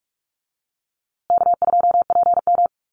Morse code, heard as a keyed tone, about 44 wpm, 700 Hz. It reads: K2PM
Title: USA Call Signs Repeated 3x 45wpm